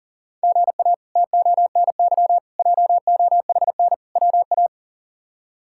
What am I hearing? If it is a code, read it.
Morse code, heard as a keyed tone, about 40 wpm, 690 Hz. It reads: GA TONY JOHN WA
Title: K1USN SST QSO Practice - 40wpm